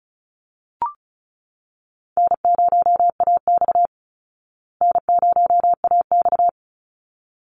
Morse code, heard as a keyed tone, about 34 wpm, 700 Hz. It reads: N0AX N0AX
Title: Call Signs Repeated 3x - 35wpm